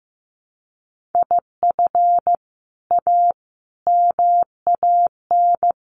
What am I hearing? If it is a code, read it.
Morse code, heard as a keyed tone, about 15 wpm, 700 Hz. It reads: IF A MAN
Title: Sentences from Top 1400 Words 15wpm